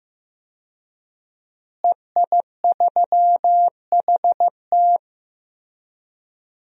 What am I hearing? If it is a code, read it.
Morse code, heard as a keyed tone, about 15 wpm, 700 Hz. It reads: EI3HT
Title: DXCC-100 Call Signs - 5 Characters - Repeated 3x - 15wpm